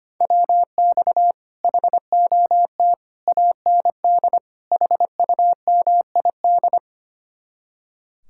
Morse code, heard as a keyed tone, about 30 wpm, 700 Hz. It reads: WX HOT AND HUMID
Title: Speed-Racing Common QSO Phrases Encore - 25wpm